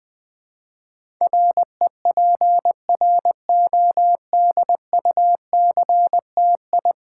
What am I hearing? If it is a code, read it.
Morse code, heard as a keyed tone, about 20 wpm, 700 Hz. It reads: REPRODUCTI
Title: Build A Word - 5 Letters At A Time - 12-Letter Words - 20wpm